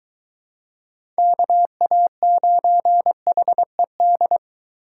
Morse code, heard as a keyed tone, about 23 wpm, 700 Hz. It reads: KA9HED